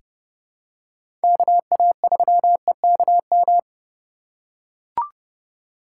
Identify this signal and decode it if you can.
Morse code, heard as a keyed tone, about 30 wpm, 700 Hz. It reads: KA3EKM
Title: USA Call Signs - 6 Characters - 3x Character Spacing - 30wpm